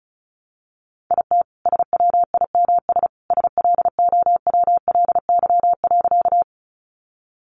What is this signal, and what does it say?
Morse code, heard as a keyed tone, about 34 wpm, 700 Hz. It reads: IT SWIMS SLOWLY.
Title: Easy Phrases 35wpm